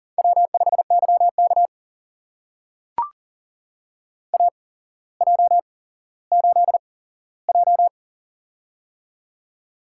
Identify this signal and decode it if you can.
Morse code, heard as a keyed tone, about 39 wpm, spaced as for 17 wpm, 700 Hz. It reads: W5YK AJ8J
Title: USA Call Signs - 4 Characters - 8x Character Spacing - 40wpm